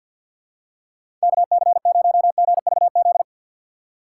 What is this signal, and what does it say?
Morse code, heard as a keyed tone, about 47 wpm, 700 Hz. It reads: KC0GVZ
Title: USA Call Signs - 6 Characters - 4x Character Spacing - 50wpm